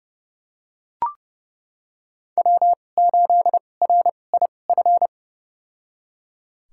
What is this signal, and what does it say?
Morse code, heard as a keyed tone, about 30 wpm, spaced as for 22 wpm, 700 Hz. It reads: W8RIF
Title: USA Call Signs - 2x Character Spacing - 30wpm